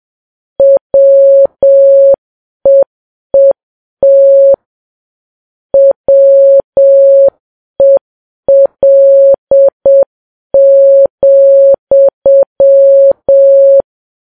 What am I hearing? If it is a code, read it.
Morse code, heard as a keyed tone, about 7 wpm, 550 Hz. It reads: WEET WEL,
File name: Files Web tls / Morse on9fv / morse7kps